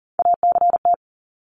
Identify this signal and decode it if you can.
Morse code, heard as a keyed tone, about 40 wpm, 700 Hz. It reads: ACT
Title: Top 300 Words 40wpm